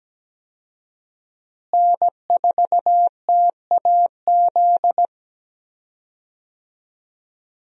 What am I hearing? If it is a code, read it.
Morse code, heard as a keyed tone, about 17 wpm, 700 Hz. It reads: N4TAZ
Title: USA Call Signs - 5 Characters - Repeated 2x - 17wpm